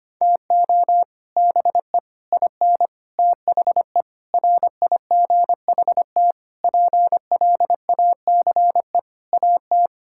Morse code, heard as a keyed tone, about 25 wpm, 700 Hz. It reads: TO BE IN THE RIGHT PLACE AT
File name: Sets of 9 Words - 25wpm